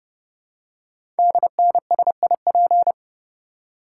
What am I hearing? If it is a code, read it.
Morse code, heard as a keyed tone, about 30 wpm, 700 Hz. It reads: DNSIP
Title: Code Groups - 5 Letters - 30wpm